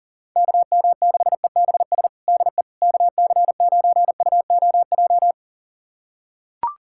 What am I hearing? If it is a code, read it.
Morse code, heard as a keyed tone, about 37 wpm, 700 Hz. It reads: KM6EBS DE KC9UOJ
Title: USA Callsign DE Callsign - 6 Characters - 40wpm